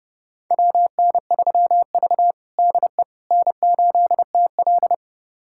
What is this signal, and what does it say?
Morse code, heard as a keyed tone, about 31 wpm, 700 Hz. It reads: WN3V DE N8TL
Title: USA Callsign DE Callsign - 4 Characters - 30wpm